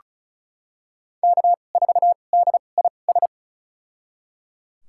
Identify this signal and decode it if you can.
Morse code, heard as a keyed tone, about 36 wpm, spaced as for 25 wpm, 700 Hz. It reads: K4DIS
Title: USA Call Signs - 5 Characters - 2x Character Spacing - 35wpm